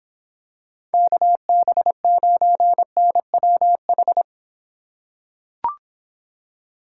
Morse code, heard as a keyed tone, about 23 wpm, 700 Hz. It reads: KB9NWH